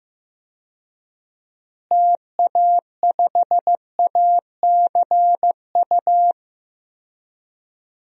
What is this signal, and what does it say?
Morse code, heard as a keyed tone, about 15 wpm, 700 Hz. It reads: TA5ACU
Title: DXCC-100 Call Signs - 6 Characters - Repeated 2x - 15wpm